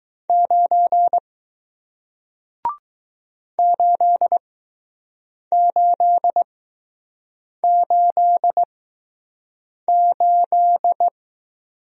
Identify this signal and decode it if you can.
Morse code, heard as a keyed tone, about 20 wpm, spaced as for 6 wpm, 700 Hz. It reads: 9 8888